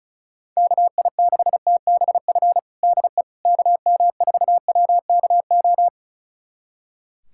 Morse code, heard as a keyed tone, about 34 wpm, 700 Hz. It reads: KI6TBF DE KM4WKO
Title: USA Callsign DE Callsign 35wpm